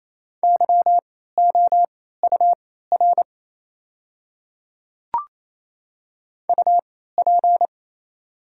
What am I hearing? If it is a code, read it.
Morse code, heard as a keyed tone, about 28 wpm, spaced as for 16 wpm, 700 Hz. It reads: YOUR UP
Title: 3x Character Spacing - Top 100 Words - 28wpm